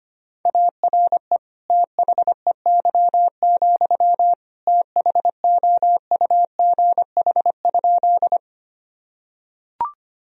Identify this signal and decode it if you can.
Morse code, heard as a keyed tone, about 25 wpm, 700 Hz. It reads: ARE THEY, THOUGH?
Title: Sentences from Top 1000 Words - Review Part 2 - 25wpm